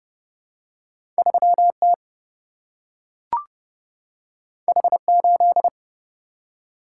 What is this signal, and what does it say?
Morse code, heard as a keyed tone, about 29 wpm, 700 Hz. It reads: 3T H8
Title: Single-Letter Single-Number 30wpm